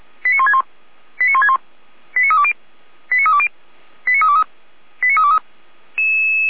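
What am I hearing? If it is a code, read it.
Five-tone selective call (ZVEI), heard as quick runs of five tones, tones of about 80 ms. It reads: 89161 89161 89320 89320 89323 89323
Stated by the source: Received on an SDR, ZVEI five-tone code 89001.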